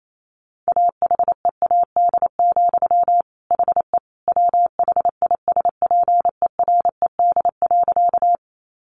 Morse code, heard as a keyed tone, about 28 wpm, 700 Hz. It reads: AHEAD, HE WHISPERED.
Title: A Sound Of Thunder 28wpm